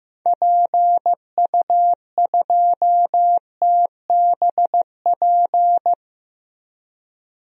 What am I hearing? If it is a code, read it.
Morse code, heard as a keyed tone, about 15 wpm, 700 Hz. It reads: PU2TBP
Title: DXCC-100 Call Signs - 6 Characters - Repeated 2x - 15wpm